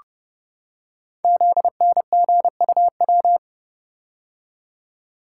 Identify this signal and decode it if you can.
Morse code, heard as a keyed tone, about 30 wpm, 700 Hz. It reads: ZNGUW
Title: Code Groups - 5 Letters - 30wpm